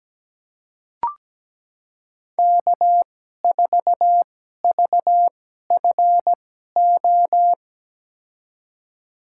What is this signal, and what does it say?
Morse code, heard as a keyed tone, about 17 wpm, spaced as for 12 wpm, 700 Hz. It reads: K4VFO
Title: USA Call Signs - 5 Characters - 2x Character Spacing - 17wpm